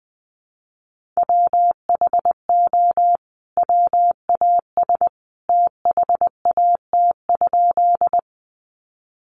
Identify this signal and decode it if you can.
Morse code, heard as a keyed tone, about 19 wpm, 700 Hz. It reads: WHO WAS THAT?
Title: Sentences from Top 100 Words 20wpm